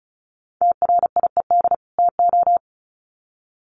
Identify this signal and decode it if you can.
Morse code, heard as a keyed tone, about 34 wpm, 700 Hz. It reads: TRIED TO
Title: Sets of 2 Words 35wpm